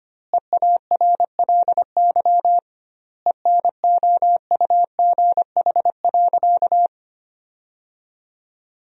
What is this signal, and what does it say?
Morse code, heard as a keyed tone, about 25 wpm, 700 Hz. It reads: EARLY ENOUGH.
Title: 2x Word-Spacing - Sentences from Top 400 Words - 25wpm